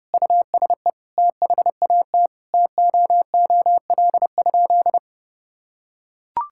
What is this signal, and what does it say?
Morse code, heard as a keyed tone, about 30 wpm, 700 Hz. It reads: USE THAT TOOL?
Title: Sentences from Top 1000 Words - Review Part 6 - 30wpm